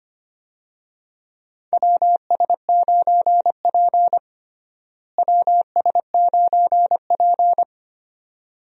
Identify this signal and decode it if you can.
Morse code, heard as a keyed tone, about 25 wpm, 700 Hz. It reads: WS9P WS9P
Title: USA Call Signs - 4 Characters - Repeated 2x - 25wpm